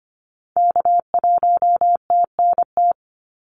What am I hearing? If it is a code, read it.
Morse code, heard as a keyed tone, about 25 wpm, 700 Hz. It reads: K1TNT
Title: NA SOTA Activator Callsigns 25wpm